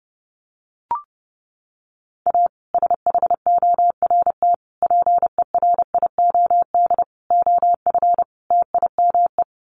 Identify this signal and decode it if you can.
Morse code, heard as a keyed tone, about 30 wpm, 700 Hz. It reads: A SHORT PERIOD OF TIME
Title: Sets of 5 Words Encore 30wpm